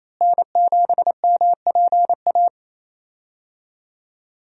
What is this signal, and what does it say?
Morse code, heard as a keyed tone, about 27 wpm, 700 Hz. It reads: N7MPA
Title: USA Call Signs - 5 Characters - Repeated 3x - 28wpm